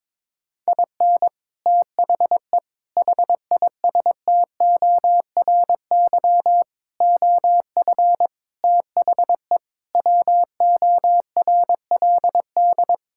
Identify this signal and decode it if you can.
Morse code, heard as a keyed tone, about 23 wpm, 700 Hz. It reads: IN THE HISTORY OF THE WORLD
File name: practice / 22wpm / Sets of 6 Words - 22wpm